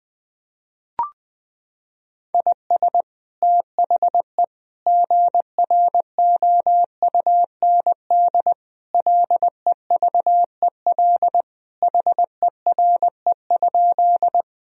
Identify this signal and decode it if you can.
Morse code, heard as a keyed tone, about 21 wpm, 700 Hz. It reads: IS THE GROUND LEVEL HERE?
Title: Sentences from Top 1000 Words - Review Part 4 - 20wpm